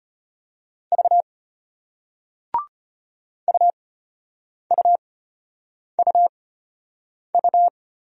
Morse code, heard as a keyed tone, about 34 wpm, spaced as for 8 wpm, 700 Hz. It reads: V UUUU